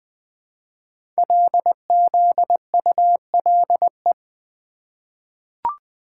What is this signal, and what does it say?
Morse code, heard as a keyed tone, about 20 wpm, 700 Hz. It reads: LZULE
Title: Code Groups - 5 Letters - 20wpm